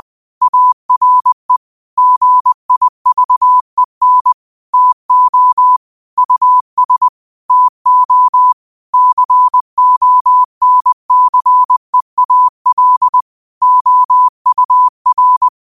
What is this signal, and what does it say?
Morse code, heard as a keyed tone, about 20 wpm, 1000 Hz. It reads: ARE GIVEN TO US TO CONCEAL OUR
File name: qotdfiles / 20wpm